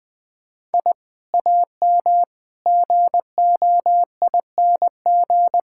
Morse code, heard as a keyed tone, about 21 wpm, 700 Hz. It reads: I AM GOING
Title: Sentences with 2-Word Phrases - 20wpm